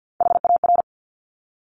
Morse code, heard as a keyed tone, about 50 wpm, 700 Hz. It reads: HAR
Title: Top Three Letter Combinations in Words 50wpm